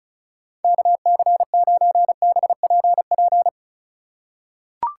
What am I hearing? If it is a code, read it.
Morse code, heard as a keyed tone, about 36 wpm, 700 Hz. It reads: KC9BPP
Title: USA Call Signs - 6 Characters - 2x Character Spacing - 35wpm